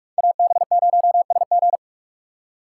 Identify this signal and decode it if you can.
Morse code, heard as a keyed tone, about 46 wpm, 700 Hz. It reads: AB0SG
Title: USA Call Signs - 5 Characters - 3x Character Spacing - 45wpm